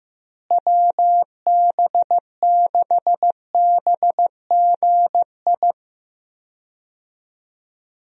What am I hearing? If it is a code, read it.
Morse code, heard as a keyed tone, about 15 wpm, 700 Hz. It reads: WB6BGI
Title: USA Call Signs - 6 Characters - Repeated 2x - 15wpm